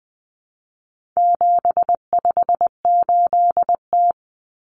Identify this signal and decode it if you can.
Morse code, heard as a keyed tone, about 20 wpm, 700 Hz. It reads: 758T
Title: Common Cut Numbers - 4-Digit 20wpm